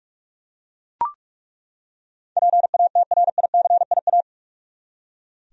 Morse code, heard as a keyed tone, about 42 wpm, 700 Hz. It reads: PATRICIA
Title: USA Common Names - 45wpm